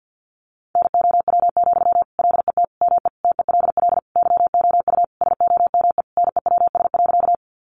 Morse code, heard as a keyed tone, about 50 wpm, 700 Hz. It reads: NOW, LET ME TELL YOU SOME NEWS.
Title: Sentences from Top 800 Words 50wpm